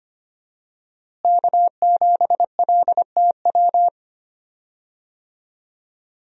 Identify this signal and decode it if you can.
Morse code, heard as a keyed tone, about 25 wpm, 700 Hz. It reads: K7LTW
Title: USA Call Signs - 5 Characters - Repeated 2x - 25wpm